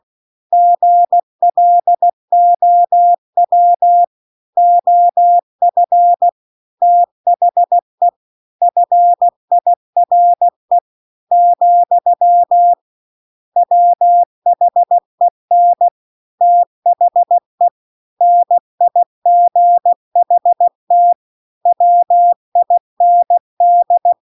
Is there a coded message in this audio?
Yes: Morse code from 0.5 to 24.1 s, heard as a keyed tone, about 16 wpm, 700 Hz. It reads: GLOW OF THE FIRE , WHEN THE NIGHT WIND